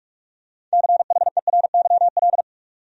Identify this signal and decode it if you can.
Morse code, heard as a keyed tone, about 46 wpm, 700 Hz. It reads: CHERYL
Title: USA Common Names - 45wpm